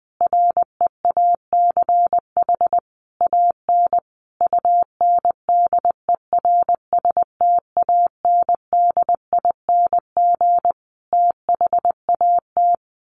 Morse code, heard as a keyed tone, about 20 wpm, 700 Hz. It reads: REACH AN UNDERSTANDING THAT
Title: Sentences from Top 1500 Words 20wpm